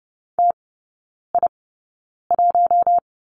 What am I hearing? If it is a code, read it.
Morse code, heard as a keyed tone, about 30 wpm, spaced as for 9 wpm, 690 Hz. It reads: TI1
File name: Lesson 16e - Letter c - Three Characters 30wpm